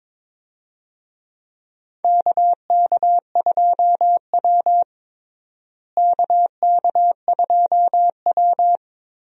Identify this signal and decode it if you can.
Morse code, heard as a keyed tone, about 21 wpm, 700 Hz. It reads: KK2W KK2W
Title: USA Call Signs - 4 Characters - Repeated 2x - 22wpm